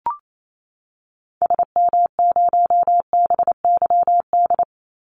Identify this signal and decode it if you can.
Morse code, heard as a keyed tone, about 28 wpm, 700 Hz. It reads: SM0BYD
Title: Call Signs - 28wpm